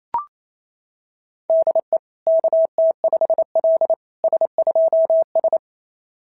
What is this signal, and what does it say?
Morse code, heard as a keyed tone, about 25 wpm, 650 Hz. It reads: DE KT5L S2S
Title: SOTA QSO Practice 28wpm